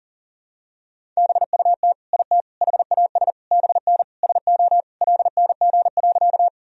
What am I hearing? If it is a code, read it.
Morse code, heard as a keyed tone, about 40 wpm, 700 Hz. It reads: BUT IT HAS BN SO LNG.
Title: Sentences from Top 100 Words and Abbreviations 40wpm